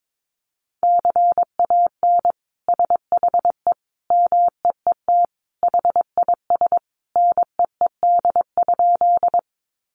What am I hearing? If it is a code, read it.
Morse code, heard as a keyed tone, about 22 wpm, 700 Hz. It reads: CAN SHE MEET HIS NEED?
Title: Sentences from Top 1100 Words 22wpm